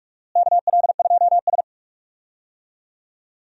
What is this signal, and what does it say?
Morse code, heard as a keyed tone, about 42 wpm, 700 Hz. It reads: KL2S
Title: USA Call Signs - 4 Characters - Repeated 2x - 45wpm